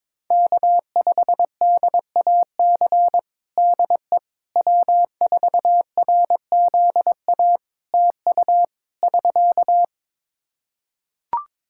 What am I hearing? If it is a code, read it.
Morse code, heard as a keyed tone, about 22 wpm, 700 Hz. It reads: K5DAC DE W4RZA TU <SK>
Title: Practice QSO - Exchange 4 - Ending - 22wpm